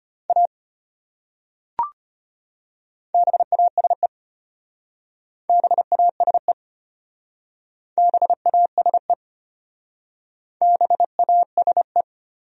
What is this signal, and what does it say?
Morse code, heard as a keyed tone, about 32 wpm, 700 Hz. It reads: A BASE BASE BASE BASE